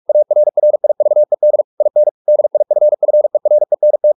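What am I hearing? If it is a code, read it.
Morse code, heard as a keyed tone, about 44 wpm, 600 Hz. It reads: ARRIVED IN DIFFERENT
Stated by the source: Morse practice at 45 wpm, the novel Never by Ken Follett in English, from CW med Gnister.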